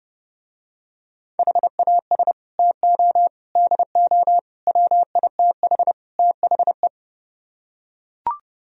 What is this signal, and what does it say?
Morse code, heard as a keyed tone, about 30 wpm, 700 Hz. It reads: HAS TO DO WITH THE